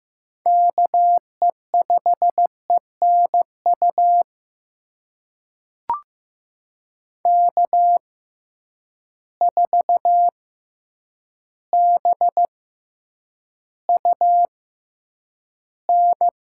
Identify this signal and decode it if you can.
Morse code, heard as a keyed tone, about 15 wpm, spaced as for 8 wpm, 700 Hz. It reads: KE5ENU K4BUN
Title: USA Call Signs - 6x Character Spacing - 15wpm